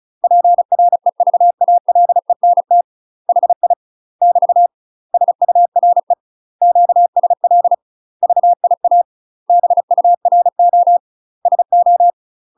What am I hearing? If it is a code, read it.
Morse code, heard as a keyed tone, about 35 wpm, 700 Hz. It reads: PREVALENT HI <BT> SURE QSL VIA BURO SO
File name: UK MASTER QSOs TRAINING / 35WPM / UK MASTER QSOs 0020